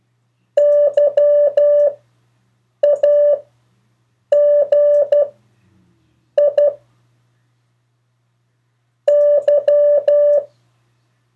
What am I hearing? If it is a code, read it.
Morse code, heard as a keyed tone, about 12 wpm, spaced as for 6 wpm, 580 Hz. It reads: YAGI Y
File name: 3of4-words-2